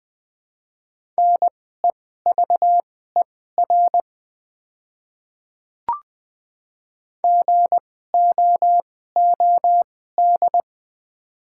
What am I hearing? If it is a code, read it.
Morse code, heard as a keyed tone, about 21 wpm, spaced as for 15 wpm, 700 Hz. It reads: NEVER GOOD
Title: 2x Character Spacing - Top 500 Words - 20wpm